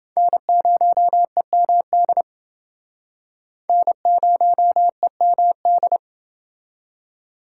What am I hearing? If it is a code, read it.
Morse code, heard as a keyed tone, about 29 wpm, 700 Hz. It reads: N0EMD N0EMD